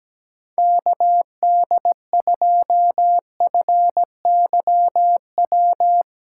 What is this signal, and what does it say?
Morse code, heard as a keyed tone, about 18 wpm, 700 Hz. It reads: KD2FYW
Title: Speed Racing - USA 2x3 Call Signs 17wpm